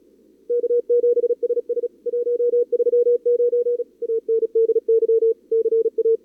Morse code, heard as a keyed tone, about 37 wpm, 440 Hz. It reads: K7SS 139 ANDY CA
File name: CWSprint / Feb2022 / k5tr-feb-na-sprint-cw-k5ot -0000